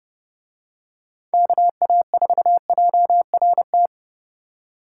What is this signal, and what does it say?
Morse code, heard as a keyed tone, about 30 wpm, 700 Hz. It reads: KA4JRT